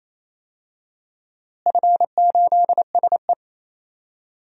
Morse code, heard as a keyed tone, about 27 wpm, 700 Hz. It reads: F8SE
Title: DXCC-100 Call Signs - 4 Characters - Repeated 2x - 28wpm